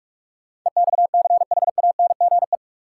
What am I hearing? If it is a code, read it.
Morse code, heard as a keyed tone, about 48 wpm, 700 Hz. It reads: EXCHANGE
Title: Sets of 5 Words - New Word Drill - 45wpm